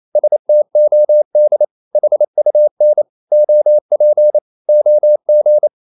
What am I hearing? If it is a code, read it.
Morse code, heard as a keyed tone, about 28 wpm, 600 Hz. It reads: STOD HUN OP OG
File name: Ferie QRQ 28wpm